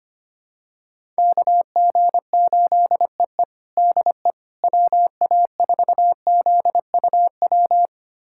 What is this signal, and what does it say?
Morse code, heard as a keyed tone, about 25 wpm, 700 Hz. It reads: KG8EE DE WA4ZUW
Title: USA Callsign DE Callsign 25wpm